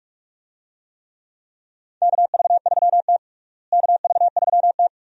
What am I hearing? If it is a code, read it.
Morse code, heard as a keyed tone, about 46 wpm, 700 Hz. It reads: KV3T KV3T
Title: USA Call Signs - 4 Characters - Repeated 2x - 45wpm